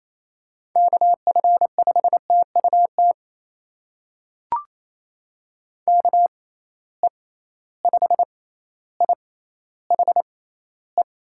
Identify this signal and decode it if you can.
Morse code, heard as a keyed tone, about 28 wpm, spaced as for 14 wpm, 700 Hz. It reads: KF5TUT KE5IHE
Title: USA Call Signs - 6 Characters - 6x Character Spacing - 28wpm